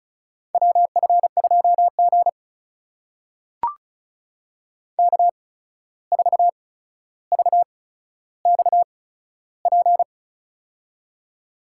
Morse code, heard as a keyed tone, about 36 wpm, spaced as for 14 wpm, 700 Hz. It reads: WF2G K4VXP
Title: USA Call Signs - 8x Character Spacing - 35wpm